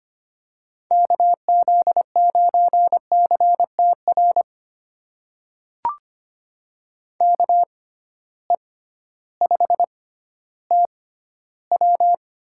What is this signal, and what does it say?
Morse code, heard as a keyed tone, about 25 wpm, spaced as for 13 wpm, 700 Hz. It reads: KZ9CTR KE5TW
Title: USA Call Signs - 6 Characters - 6x Character Spacing - 25wpm